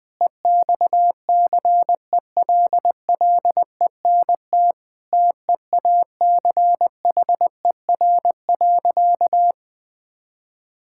Morse code, heard as a keyed tone, about 20 wpm, 700 Hz. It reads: EXCELLENT TEACHER.
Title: Sentences from Top 1000 Words - Review Part 6 - 20wpm